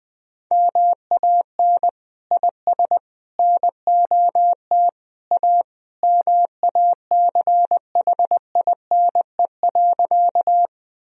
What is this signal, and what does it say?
Morse code, heard as a keyed tone, about 20 wpm, 700 Hz. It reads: MAN IS NOT A MACHINE.
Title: Sentences from Top 1000 Words - Review Part 2 - 20wpm